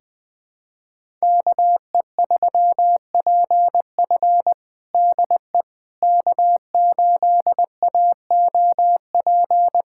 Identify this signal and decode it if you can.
Morse code, heard as a keyed tone, about 20 wpm, 700 Hz. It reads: KE3PF DE K8AOP
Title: USA Callsign DE Callsign - 5 Characters - 20wpm